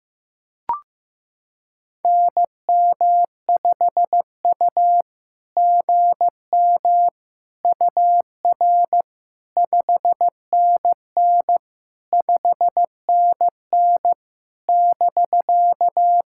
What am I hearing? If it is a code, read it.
Morse code, heard as a keyed tone, about 15 wpm, 700 Hz. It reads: NM5U GM UR 5NN 5NN <BK>
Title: SOTA QSO Practice 15wpm